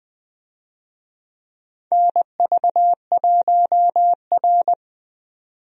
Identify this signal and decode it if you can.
Morse code, heard as a keyed tone, about 20 wpm, 700 Hz. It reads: NV1R
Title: USA Call Signs - 4 Characters - 20wpm